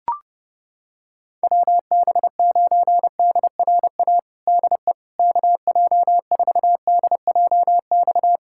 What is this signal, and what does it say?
Morse code, heard as a keyed tone, about 30 wpm, 700 Hz. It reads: WB9DRA DE KJ4DJX
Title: USA Callsign DE Callsign - 6 Characters - 30wpm